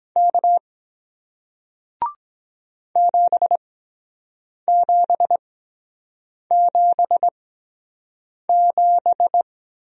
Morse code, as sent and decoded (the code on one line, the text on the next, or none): -.- / --... --... --... --...
K 7777